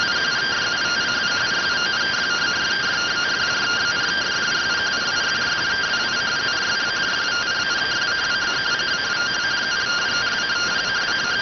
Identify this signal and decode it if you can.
Two-tone teleprinter signal, about 170 Hz shift.